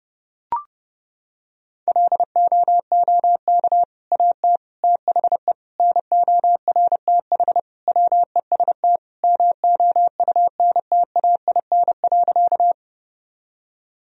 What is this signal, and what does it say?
Morse code, heard as a keyed tone, about 30 wpm, 700 Hz. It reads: LOOK AT THE NORTH WEST MOUNTAIN.
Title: Sentences from Top 1000 Words - Review Part 2 - 30wpm